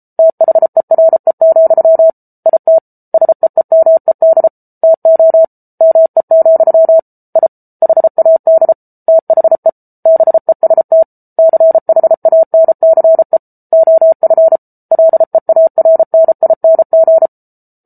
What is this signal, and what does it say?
Morse code, heard as a keyed tone, about 34 wpm, 650 Hz. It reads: THERE, IT SEEMED TO ME, I HAD THE BEST CHANCE OF LEARNING